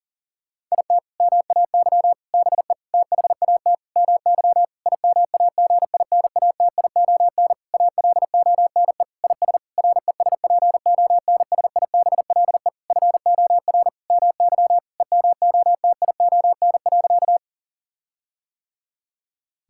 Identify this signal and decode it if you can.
Morse code, heard as a keyed tone, about 40 wpm, 700 Hz. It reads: IT MAY BE THAT MY IMAGINATION ALONE IS RESPONSIBLE FOR MY EMOTION.
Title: Sentences from Top 1600 Words 40wpm